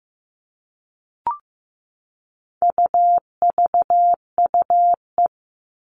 Morse code, heard as a keyed tone, about 15 wpm, 700 Hz. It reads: UVUE
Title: All Cut Numbers - 4-Digit 15wpm